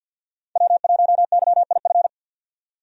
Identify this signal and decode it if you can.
Morse code, heard as a keyed tone, about 52 wpm, 700 Hz. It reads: W1YIF
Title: USA Call Signs - 5 Characters - 50wpm